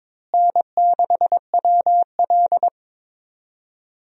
Morse code, heard as a keyed tone, about 21 wpm, 700 Hz. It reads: N6WL
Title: USA Call Signs - 4 Characters - Repeated 3x - 22wpm